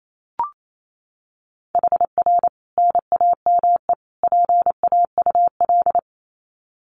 Morse code, heard as a keyed tone, about 28 wpm, 700 Hz. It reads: HR NAME PAUL
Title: Common QSO Phrases 28wpm